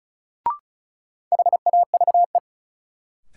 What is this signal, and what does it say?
Morse code, heard as a keyed tone, about 34 wpm, 700 Hz. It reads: HAVE
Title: Sentences from Top 100 Words - New Word Drill - 35wpm